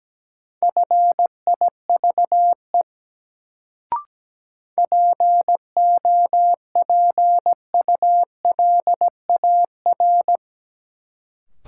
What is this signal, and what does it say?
Morse code, heard as a keyed tone, about 17 wpm, 700 Hz. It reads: FIVE POPULAR
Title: Sets of 5 Words - New Word Drill - 17wpm